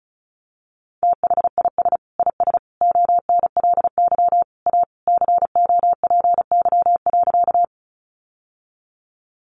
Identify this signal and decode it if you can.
Morse code, heard as a keyed tone, about 34 wpm, 700 Hz. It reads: THIS IS ONLY A COPY.
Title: Sentences from Top 1400 Words 35wpm